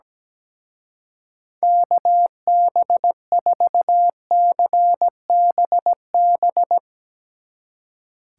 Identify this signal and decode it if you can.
Morse code, heard as a keyed tone, about 20 wpm, 700 Hz. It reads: KB4CBB
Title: Speed Racing - USA 2x3 Call Signs 17wpm